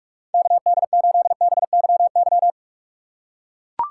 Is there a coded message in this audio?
Yes: Morse code from 0.3 to 2.5 s, heard as a keyed tone, about 44 wpm, 700 Hz. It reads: KD8BYY